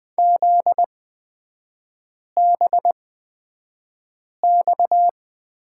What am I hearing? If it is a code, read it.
Morse code, heard as a keyed tone, about 20 wpm, spaced as for 5 wpm, 700 Hz. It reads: ZBX
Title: 243 Lesson 39e - Letter z - Three Characters 20wpm